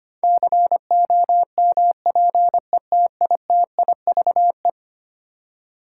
Morse code, heard as a keyed tone, about 25 wpm, 700 Hz. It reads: COMPETITIVE
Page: morse-code-ninja-practice-25wpm